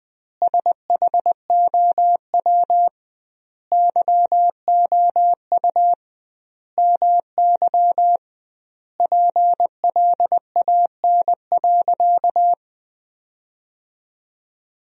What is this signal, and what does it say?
Morse code, heard as a keyed tone, about 20 wpm, 700 Hz. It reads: SHOW YOU MY PLAN.
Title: 2x Word-Spacing - Sentences from Top 500 Words - 20wpm